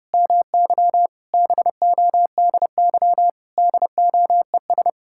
Morse code, heard as a keyed tone, about 30 wpm, 700 Hz. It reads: MY BODY DOES
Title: Sentences from Top 1000 Words - Review Part 6 - 30wpm